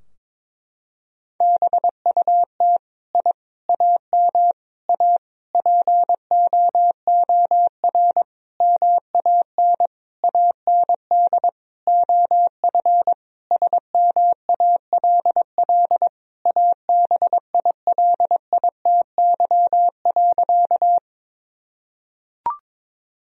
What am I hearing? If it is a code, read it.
Morse code, heard as a keyed tone, about 21 wpm, 700 Hz. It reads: BUT I AM A POOR MAN AND OF SMALL ABILITY.
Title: Sentences from Top 1000 Words - Review Part 3 - 22wpm